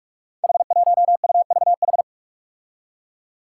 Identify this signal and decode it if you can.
Morse code, heard as a keyed tone, about 44 wpm, 700 Hz. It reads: H1UVH
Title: Five Challenging Letter-Number Combinations 45wpm